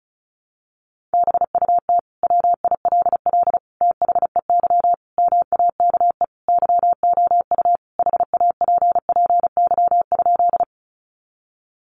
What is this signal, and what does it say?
Morse code, heard as a keyed tone, about 34 wpm, 700 Hz. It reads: BUT WILL THEY MAKE YOU HAPPY?
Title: Sentences from Top 1000 Words 35wpm